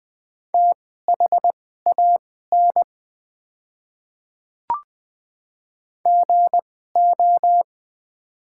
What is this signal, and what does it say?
Morse code, heard as a keyed tone, about 20 wpm, spaced as for 15 wpm, 700 Hz. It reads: THAN GO
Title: 2x Character Spacing - Top 200 Words - 20wpm